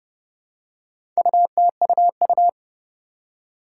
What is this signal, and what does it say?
Morse code, heard as a keyed tone, about 29 wpm, 700 Hz. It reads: UTUU
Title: Four Challenging Letter-Number Combinations 30wpm